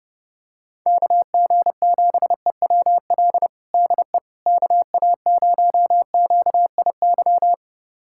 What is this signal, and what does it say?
Morse code, heard as a keyed tone, about 29 wpm, 700 Hz. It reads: KG7EWL DE KA0QIY
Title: USA Callsign DE Callsign - 6 Characters - 30wpm